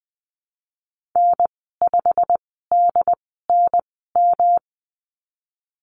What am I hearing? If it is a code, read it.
Morse code, heard as a keyed tone, about 20 wpm, spaced as for 15 wpm, 700 Hz. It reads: N5DNM
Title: 154 Lesson 25g - Letter p - Call Signs 20wpm